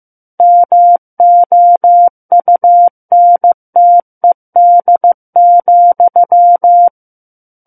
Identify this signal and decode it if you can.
Morse code, heard as a keyed tone, about 15 wpm, 700 Hz. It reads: MOUNTED,
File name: Book One / 06-Chapter 6 - 15 WPM